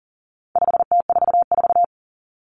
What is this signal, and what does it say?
Morse code, heard as a keyed tone, about 38 wpm, 700 Hz. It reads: 5T44
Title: Common Cut Numbers - 4-Digit 40wpm